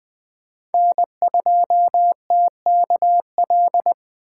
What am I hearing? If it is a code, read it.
Morse code, heard as a keyed tone, about 20 wpm, 700 Hz. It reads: N2TKL
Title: USA Call Signs - 5 Characters - 6x Character Spacing - 20wpm